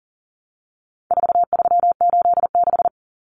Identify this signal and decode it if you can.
Morse code, heard as a keyed tone, about 40 wpm, 700 Hz. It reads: 4386
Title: Numbers - 4-Digit 40wpm